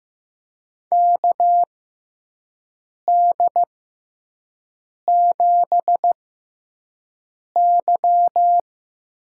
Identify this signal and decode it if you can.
Morse code, heard as a keyed tone, about 15 wpm, spaced as for 5 wpm, 700 Hz. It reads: KD7Y
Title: USA Call Signs - 6 Characters - 6x Character Spacing - 15wpm